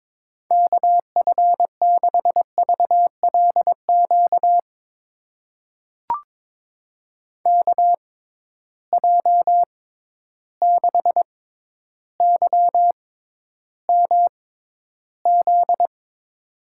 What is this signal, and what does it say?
Morse code, heard as a keyed tone, about 22 wpm, spaced as for 11 wpm, 700 Hz. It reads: KF6VLQ KJ6YMZ
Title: USA Call Signs - 6 Characters - 6x Character Spacing - 22wpm